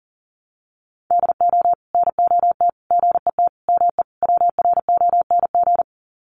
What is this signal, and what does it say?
Morse code, heard as a keyed tone, about 40 wpm, 700 Hz. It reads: DO NOT GET ME WRONG
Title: Sets of 5 Words Encore 40wpm